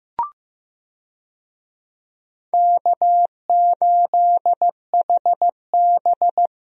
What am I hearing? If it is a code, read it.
Morse code, heard as a keyed tone, about 15 wpm, 700 Hz. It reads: K8HB
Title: USA Call Signs - 4 Characters - Repeated 2x - 15wpm